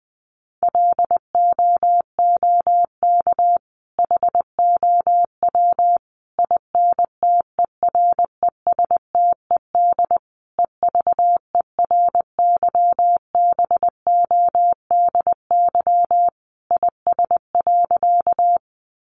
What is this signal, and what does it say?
Morse code, heard as a keyed tone, about 20 wpm, 700 Hz. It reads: LOOK HOW INTERESTED EVERYBODY IS.
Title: Sentences from Top 1400 Words 20wpm